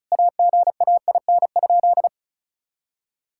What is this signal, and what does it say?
Morse code, heard as a keyed tone, about 36 wpm, 700 Hz. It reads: AGAIN?
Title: Sentences from Top 1600 Words 35wpm